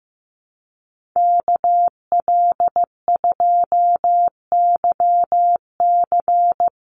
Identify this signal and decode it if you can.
Morse code, heard as a keyed tone, about 15 wpm, 700 Hz. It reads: KL2YC
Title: USA Call Signs Repeated 3x 15wpm